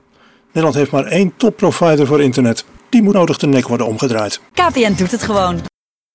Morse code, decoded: none